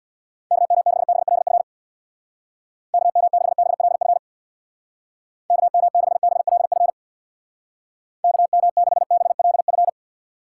Morse code, heard as a keyed tone, about 64 wpm, 700 Hz. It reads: KM6BLF KM6BLF KM6BLF KM6BLF